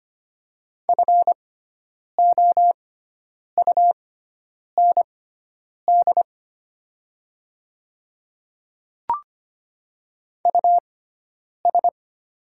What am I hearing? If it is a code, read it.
Morse code, heard as a keyed tone, about 25 wpm, spaced as for 9 wpm, 700 Hz. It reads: FOUND US
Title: 6x Character Spacing - Top 300 Words - 25wpm